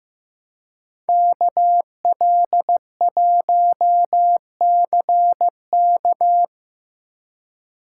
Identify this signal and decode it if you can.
Morse code, heard as a keyed tone, about 18 wpm, 700 Hz. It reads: KL1CK
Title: Speed Racing - USA 2x2 Call Signs 15wpm